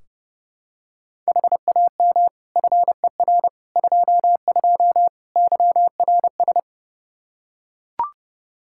Morse code, heard as a keyed tone, about 32 wpm, 700 Hz. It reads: HAM FER 22 YRS
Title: Practice QSO - Exchange 3 - More Info - 30wpm